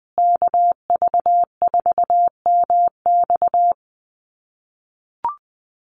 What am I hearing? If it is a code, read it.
Morse code, heard as a keyed tone, about 20 wpm, 700 Hz. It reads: KV4MX
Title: USA Call Signs Repeated 3x 20wpm